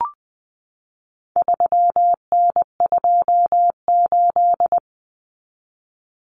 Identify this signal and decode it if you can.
Morse code, heard as a keyed tone, about 20 wpm, 700 Hz. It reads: 3N28
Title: Common Cut Numbers - 4-Digit 20wpm